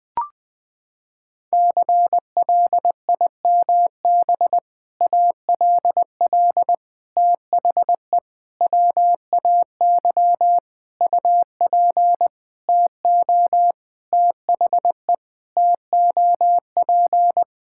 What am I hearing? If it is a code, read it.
Morse code, heard as a keyed tone, about 21 wpm, 700 Hz. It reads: CLIMB ALL THE WAY UP TO THE TOP
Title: Sentences with 6-Word Phrases - 20wpm